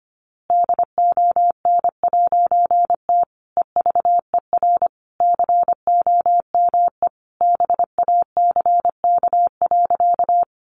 Morse code, heard as a keyed tone, about 25 wpm, 700 Hz. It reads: DON'T EVER COME BACK.
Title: A Sound Of Thunder 25wpm